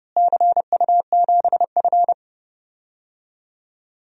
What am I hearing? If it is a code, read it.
Morse code, heard as a keyed tone, about 30 wpm, 700 Hz. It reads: CU7F
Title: DXCC-100 Call Signs - 4 Characters - Repeated 2x - 30wpm